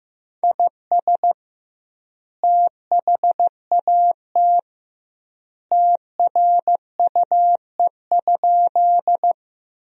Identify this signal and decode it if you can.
Morse code, heard as a keyed tone, about 15 wpm, 700 Hz. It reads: IS THAT TRUE?
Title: 2x Word-Spacing - Sentences from Top 400 Words - 15wpm